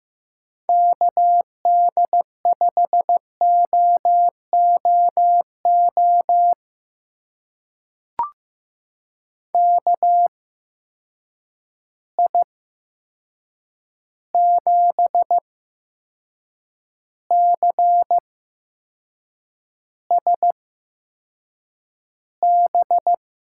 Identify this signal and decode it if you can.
Morse code, heard as a keyed tone, about 15 wpm, spaced as for 6 wpm, 700 Hz. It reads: KD5OOO KI7CSB
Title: USA Call Signs - 6 Characters - 8x Character Spacing - 15wpm